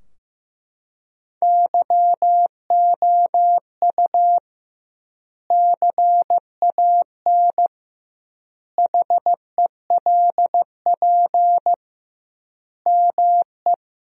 Morse code, heard as a keyed tone, about 15 wpm, 700 Hz. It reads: YOU CAN HELP ME
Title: 2x Word-Spacing - Sentences from Top 200 Words - 15wpm